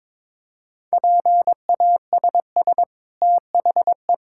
Morse code, heard as a keyed tone, about 22 wpm, 690 Hz. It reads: PASS THE
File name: practice / 22wpm / Easy Sentences with Hints - 22wpm